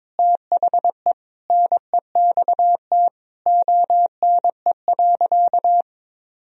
Morse code, heard as a keyed tone, about 22 wpm, 700 Hz. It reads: THE NEXT ONE.
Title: Sentences from Top 400 Words and Abbreviations 22wpm